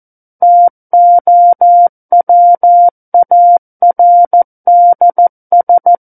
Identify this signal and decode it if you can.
Morse code, heard as a keyed tone, about 14 wpm, 700 Hz. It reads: TOWARDS
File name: Book One / 05-Chapter 5 - 14 WPM